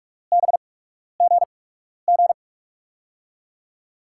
Morse code, heard as a keyed tone, about 44 wpm, spaced as for 12 wpm, 700 Hz. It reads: BGG
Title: USA Call Signs - 8x Character Spacing - 45wpm